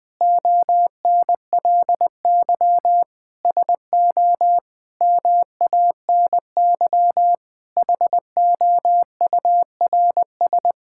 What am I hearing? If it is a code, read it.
Morse code, heard as a keyed tone, about 21 wpm, 700 Hz. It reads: ONLY SO MANY HOURS
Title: Sentences with 9-Word Phrases - 20wpm